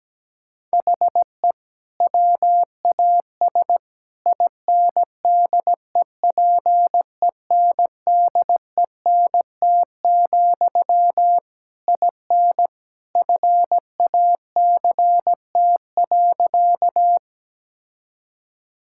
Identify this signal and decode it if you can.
Morse code, heard as a keyed tone, about 17 wpm, 700 Hz. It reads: HE WAS INDEPENDENT, IN FACT.
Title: Sentences from Top 1000 Words - Review Part 6 - 17wpm